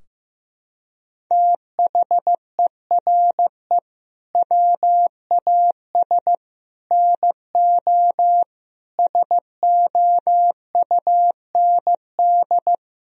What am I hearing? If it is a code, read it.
Morse code, heard as a keyed tone, about 15 wpm, 700 Hz. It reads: THERE WAS NO SOUND
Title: Sentences from Top 1000 Words - Review Part 3 - 15wpm